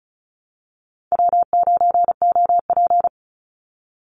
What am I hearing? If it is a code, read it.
Morse code, heard as a keyed tone, about 34 wpm, 700 Hz. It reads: W9OP
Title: Call Signs Repeated 2x - 35wpm